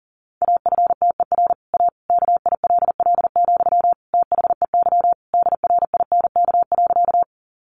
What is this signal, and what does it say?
Morse code, heard as a keyed tone, about 40 wpm, 700 Hz. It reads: AFTER A KILL, THEY DRINK.
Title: Sentences from Top 1500 Words 40wpm